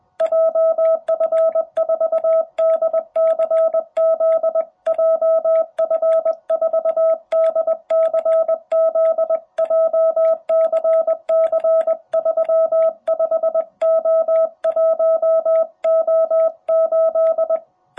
Morse code, heard as a keyed tone, about 20 wpm, 660 Hz. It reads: JF4DCZ JF4DCZ JCC35O1O8
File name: jh8cbh / 2021cw